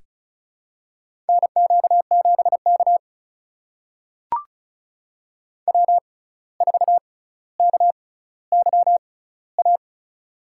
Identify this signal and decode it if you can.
Morse code, heard as a keyed tone, about 34 wpm, spaced as for 17 wpm, 700 Hz. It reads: NQ7K W4KYA
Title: USA Call Signs - 6x Character Spacing - 35wpm